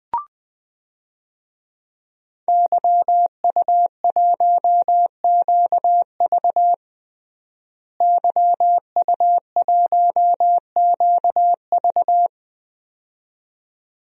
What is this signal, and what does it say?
Morse code, heard as a keyed tone, about 20 wpm, 700 Hz. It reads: YU1QV YU1QV
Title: DXCC-100 Call Signs - 5 Characters - Repeated 2x - 20wpm